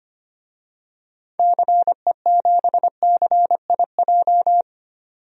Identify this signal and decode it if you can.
Morse code, heard as a keyed tone, about 25 wpm, 700 Hz. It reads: CE7CIJ
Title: DXCC-100 Call Signs - 6 Characters - 25wpm